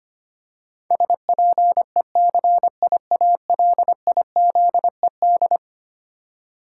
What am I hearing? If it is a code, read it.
Morse code, heard as a keyed tone, about 25 wpm, 700 Hz. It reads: SPECIALIZED
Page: morse-code-ninja-practice-25wpm